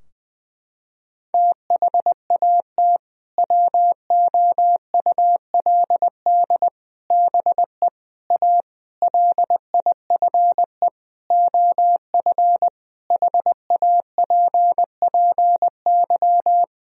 Morse code, heard as a keyed tone, about 20 wpm, 700 Hz. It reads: THAT WOULD BE A LIFE OF HAPPY
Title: Sentences from Top 1000 Words - Review Part 4 - 20wpm